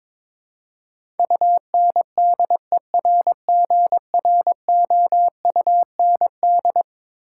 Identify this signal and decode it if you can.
Morse code, heard as a keyed tone, about 22 wpm, 700 Hz. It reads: UNDERGROUND
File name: practice / 22wpm / Eleven Letter Words - 22wpm